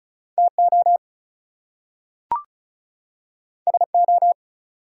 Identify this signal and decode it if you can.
Morse code, heard as a keyed tone, about 36 wpm, 700 Hz. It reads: TO SO
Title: Two Letter Words - 35wpm